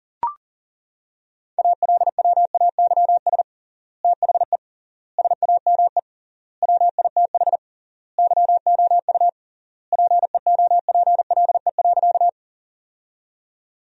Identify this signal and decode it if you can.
Morse code, heard as a keyed tone, about 40 wpm, 700 Hz. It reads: ALWAYS THE SAME WITH YOU PEOPLE.
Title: 3x Word-Spacing - Sentences from Top 300 Words - 40wpm